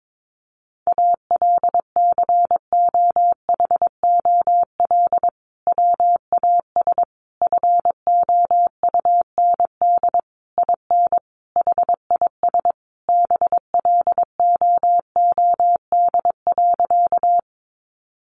Morse code, heard as a keyed tone, about 22 wpm, 700 Hz. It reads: ALCOHOL WAS FOUND IN HIS BLOOD.
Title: Sentences from Top 1400 Words 22wpm